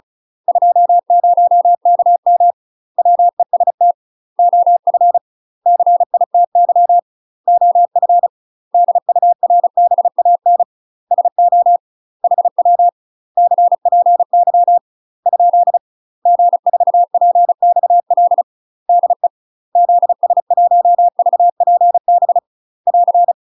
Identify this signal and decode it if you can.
Morse code, heard as a keyed tone, about 37 wpm, 700 Hz. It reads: 20KM WEST OF CITY OF DURBAN SO HW CPY ? G4PXL DE ZS1VPB <AR>
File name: UK MASTER QSOs TRAINING / 35WPM / UK MASTER QSOs 0021